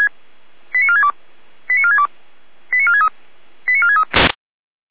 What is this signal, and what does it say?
Five-tone selective call (ZVEI), heard as quick runs of five tones, tones of about 80 ms. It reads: 89462 89462 89463 89463